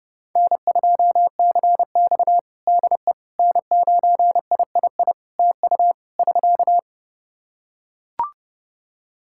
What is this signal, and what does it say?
Morse code, heard as a keyed tone, about 30 wpm, 700 Hz. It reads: N2CX DE N9III TU <SK>
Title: K1USN SST QSO Practice 30wpm